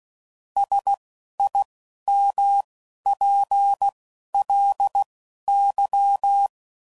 Morse code, heard as a keyed tone, about 16 wpm, spaced as for 11 wpm, 800 Hz. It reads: SIMPLY
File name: CW Code Groups @10WPM / Technical Subjects / XC